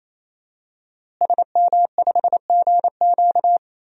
Morse code, heard as a keyed tone, about 28 wpm, 700 Hz. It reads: SM5GQ
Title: DXCC-100 Call Signs - 5 Characters - Repeated 3x - 28wpm